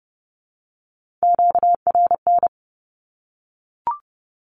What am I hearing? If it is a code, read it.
Morse code, heard as a keyed tone, about 30 wpm, 700 Hz. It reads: QRN
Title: Q-Codes Repeated 3x 30wpm